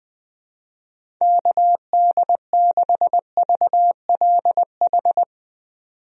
Morse code, heard as a keyed tone, about 21 wpm, 700 Hz. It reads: KD6VLH
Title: USA Call Signs - 6 Characters - 20wpm